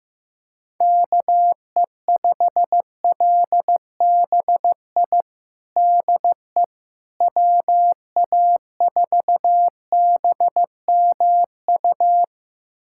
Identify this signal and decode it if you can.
Morse code, heard as a keyed tone, about 16 wpm, 700 Hz. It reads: KE5LBI DE WA4BMU
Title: USA Callsign DE Callsign - 6 Characters - 15wpm